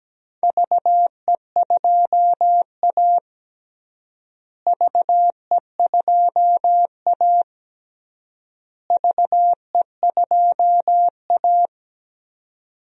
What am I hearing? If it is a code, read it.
Morse code, heard as a keyed tone, about 17 wpm, 700 Hz. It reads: VE2A VE2A VE2A
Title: DXCC-100 Call Signs - 4 Characters - Repeated 3x - 17wpm